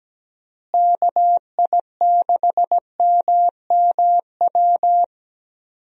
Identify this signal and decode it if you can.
Morse code, heard as a keyed tone, about 17 wpm, 700 Hz. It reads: KI6MMW
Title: USA Call Signs - 6 Characters - Repeated 2x - 17wpm